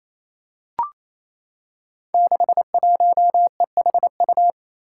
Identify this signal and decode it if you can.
Morse code, heard as a keyed tone, about 29 wpm, 700 Hz. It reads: 61EHU
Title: Five Challenging Letter-Number Combinations 28wpm